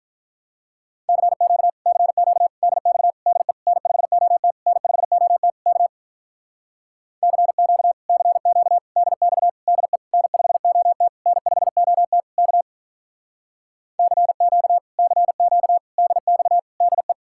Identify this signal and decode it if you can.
Morse code, heard as a keyed tone, about 48 wpm, 700 Hz. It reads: CQ CQ DX DE N5OT N5OT K CQ CQ DX DE N5OT N5OT K CQ CQ DX DE